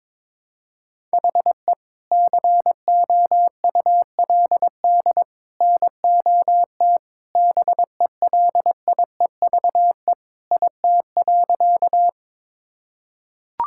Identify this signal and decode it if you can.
Morse code, heard as a keyed tone, about 22 wpm, 700 Hz. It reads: HE COULD NOT BELIEVE IT.
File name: practice / 22wpm / Sentences with 3-Word Phrases - 22wpm